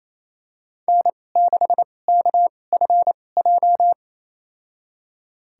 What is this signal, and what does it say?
Morse code, heard as a keyed tone, about 28 wpm, spaced as for 20 wpm, 700 Hz. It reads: N6KFJ
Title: USA Call Signs - 5 Characters - 2x Character Spacing - 28wpm